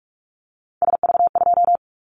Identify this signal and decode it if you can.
Morse code, heard as a keyed tone, about 44 wpm, 700 Hz. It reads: SV2
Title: Callsign Prefixes for Top 50 DXCC - 45wpm